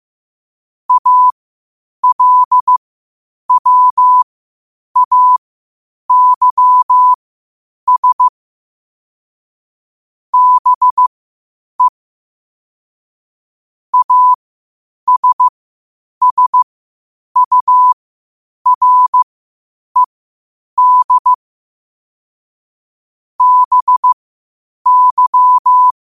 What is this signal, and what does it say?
Morse code, heard as a keyed tone, about 15 wpm, spaced as for 8 wpm, 1000 Hz. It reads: ALWAYS BE ASSURED BY
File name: qotdfiles / 8wpm